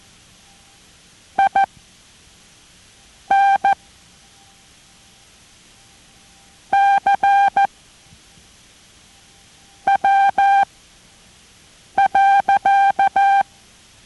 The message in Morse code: INCW.